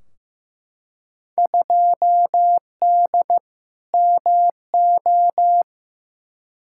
Morse code, heard as a keyed tone, about 15 wpm, 700 Hz. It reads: 2D MO
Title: Field Day Exchanges - 15wpm